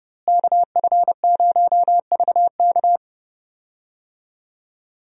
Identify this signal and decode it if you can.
Morse code, heard as a keyed tone, about 32 wpm, 700 Hz. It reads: KF0VK
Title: USA Call Signs - 5 Characters - Repeated 3x - 30wpm